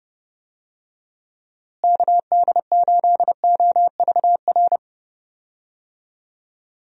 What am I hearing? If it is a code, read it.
Morse code, heard as a keyed tone, about 29 wpm, 700 Hz. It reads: KD8OVR
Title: USA Call Signs - 6 Characters - Repeated 2x - 30wpm